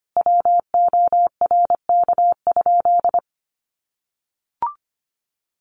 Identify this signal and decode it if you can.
Morse code, heard as a keyed tone, about 25 wpm, 700 Hz. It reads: WORK?
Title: Sentences from Top 1400 Words 25wpm